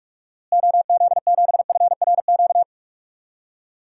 Morse code, heard as a keyed tone, about 41 wpm, 700 Hz. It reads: OZ7FRQ
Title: DXCC-100 Call Signs - 6 Characters - Repeated 3x - 45wpm